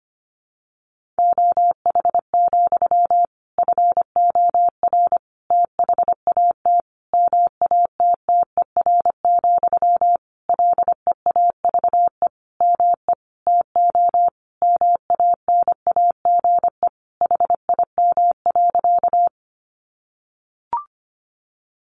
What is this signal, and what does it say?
Morse code, heard as a keyed tone, about 25 wpm, 700 Hz. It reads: OH, FOR THAT MATTER, LEAVE ME TO MANAGE HIM.
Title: Sentences from Top 1400 Words 25wpm